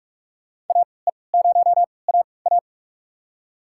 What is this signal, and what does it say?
Morse code, heard as a keyed tone, about 47 wpm, spaced as for 26 wpm, 700 Hz. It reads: AE0AA
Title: USA Call Signs - 5 Characters - 3x Character Spacing - 45wpm